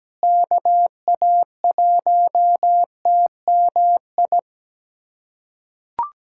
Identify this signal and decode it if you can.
Morse code, heard as a keyed tone, about 17 wpm, 700 Hz. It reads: KA1TMI
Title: USA Callsign DE Callsign - 6 Characters - 17wpm